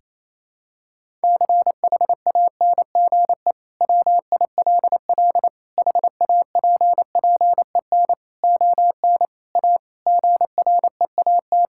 Morse code, heard as a keyed tone, about 28 wpm, 700 Hz. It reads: CHANGE WILL HAPPEN ON A GREAT
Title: Sentences from Top 1000 Words - Review Part 5 - 28wpm